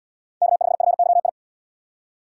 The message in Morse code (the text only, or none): K5FYI